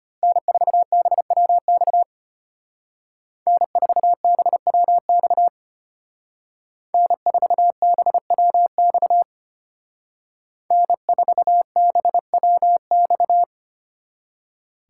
Morse code, heard as a keyed tone, about 32 wpm, 700 Hz. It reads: N4BWX N4BWX N4BWX N4BWX